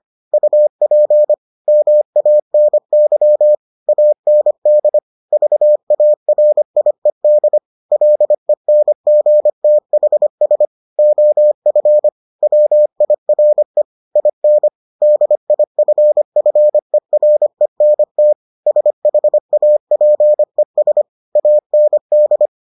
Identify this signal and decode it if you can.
Morse code, heard as a keyed tone, about 25 wpm, 600 Hz. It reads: UP MANY AND VARIED LENGTHS OF WIRE IN DIFFERENT SHAPES AND